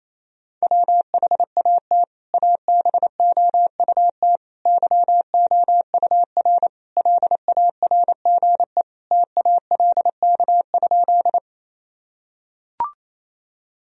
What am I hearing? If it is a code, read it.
Morse code, heard as a keyed tone, about 28 wpm, 700 Hz. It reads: WHAT ABOUT YOUR LARGE TALK?
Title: Sentences from Top 1000 Words - Review Part 1 - 28wpm